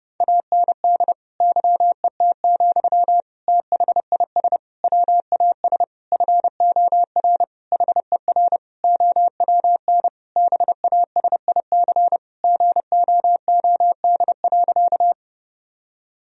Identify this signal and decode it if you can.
Morse code, heard as a keyed tone, about 30 wpm, 700 Hz. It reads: AND YET, THIS WAS FOR HER OWN BASIC GOOD.
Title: Sentences from Top 1000 Words - Review Part 3 - 30wpm